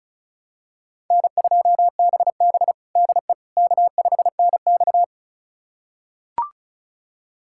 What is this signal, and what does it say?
Morse code, heard as a keyed tone, about 34 wpm, 700 Hz. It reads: N2BB DE K5NX
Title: USA Callsign DE Callsign - 4 Characters - 35wpm